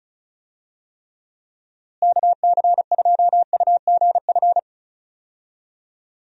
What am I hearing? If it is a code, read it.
Morse code, heard as a keyed tone, about 36 wpm, 700 Hz. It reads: KC2UGF
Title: USA Call Signs - 6 Characters - Repeated 2x - 35wpm